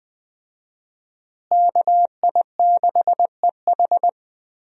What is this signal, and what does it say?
Morse code, heard as a keyed tone, about 20 wpm, 700 Hz. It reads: KI6EH
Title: USA Call Signs - 5 Characters - Repeated 3x - 20wpm